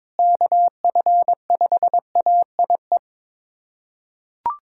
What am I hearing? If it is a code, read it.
Morse code, heard as a keyed tone, about 22 wpm, 700 Hz. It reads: KF5AIE
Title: USA Callsign DE Callsign - 6 Characters - 22wpm